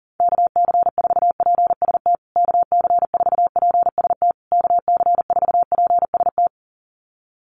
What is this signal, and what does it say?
Morse code, heard as a keyed tone, about 40 wpm, 700 Hz. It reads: KC4PST KC4PST KC4PST
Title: USA Call Signs Repeated 3x 40wpm